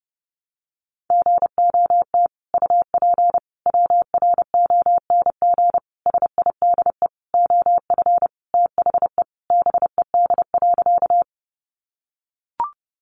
Morse code, heard as a keyed tone, about 30 wpm, 700 Hz. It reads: GOT UP WRONG SIDE OF THE BED.
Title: Sentences from Top 900 Words 30wpm